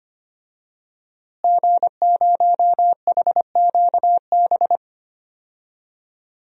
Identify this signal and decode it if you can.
Morse code, heard as a keyed tone, about 25 wpm, 700 Hz. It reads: G0HQB
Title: DXCC-100 Call Signs - 5 Characters - Repeated 3x - 25wpm